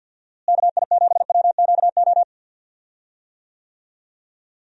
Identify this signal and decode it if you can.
Morse code, heard as a keyed tone, about 50 wpm, 700 Hz. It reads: KI7WQO
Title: USA Call Signs - 6 Characters - Repeated 3x - 50wpm